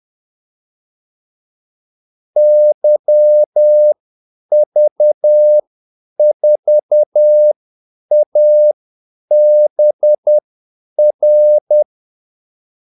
Morse code, heard as a keyed tone, about 10 wpm, spaced as for 8 wpm, 600 Hz. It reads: YV4ABR
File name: Callsigns-10-8wpm